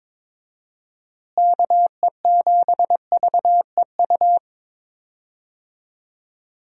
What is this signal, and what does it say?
Morse code, heard as a keyed tone, about 22 wpm, 700 Hz. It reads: KE7VEU
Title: USA Call Signs - 6 Characters - Repeated 2x - 22wpm